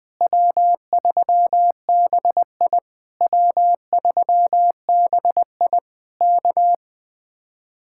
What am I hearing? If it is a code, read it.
Morse code, heard as a keyed tone, about 20 wpm, 700 Hz. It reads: W3BI W3BI K
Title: Speed-Racing Common QSO Phrases Encore - 20wpm